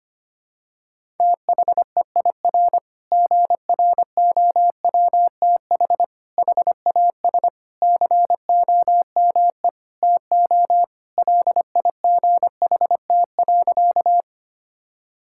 Morse code, heard as a keyed tone, about 25 wpm, 700 Hz. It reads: THEIR GROWTH HAS COME TO LIGHT.
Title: Sentences from Top 1000 Words - Review Part 5 - 25wpm